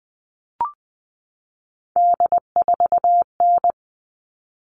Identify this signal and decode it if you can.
Morse code, heard as a keyed tone, about 20 wpm, 700 Hz. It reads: D4N
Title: All Cut Numbers - 3-Digit 20wpm